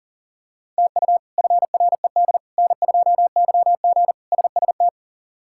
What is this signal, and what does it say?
Morse code, heard as a keyed tone, about 37 wpm, 700 Hz. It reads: TU FRED N2YG SST
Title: K1USN SST QSO Practice - 40wpm